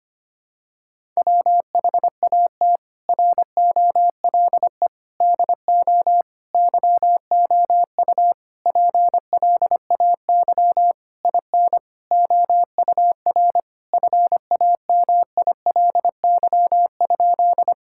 Morse code, heard as a keyed tone, about 25 wpm, 700 Hz. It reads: WHAT ROLE DO YOU PLAY IN OUR FAMILY?
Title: Sentences from Top 1000 Words - Review Part 6 - 25wpm